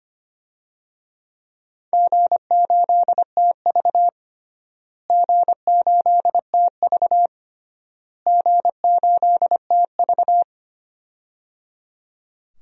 Morse code, heard as a keyed tone, about 25 wpm, 700 Hz. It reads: G8TV G8TV G8TV
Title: DXCC-100 Call Signs - 4 Characters - Repeated 3x - 25wpm